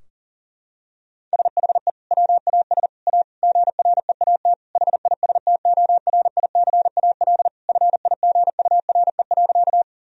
Morse code, heard as a keyed tone, about 40 wpm, 700 Hz. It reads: SHE WAS A GREAT HISTORICAL FIGURE.
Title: Sentences from Top 1000 Words - Review Part 4 - 40wpm